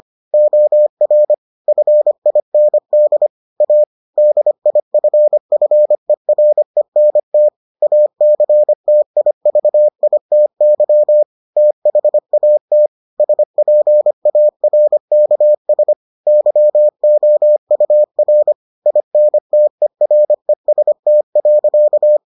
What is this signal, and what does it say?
Morse code, heard as a keyed tone, about 25 wpm, 600 Hz. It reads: OR FIND A DIFFERENT ACTIVITY THAT SPARKS YOUR INTEREST.